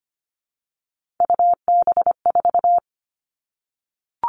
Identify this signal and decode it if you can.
Morse code, heard as a keyed tone, about 25 wpm, 700 Hz. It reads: UB4
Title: All Cut Numbers - 3-Digit 25wpm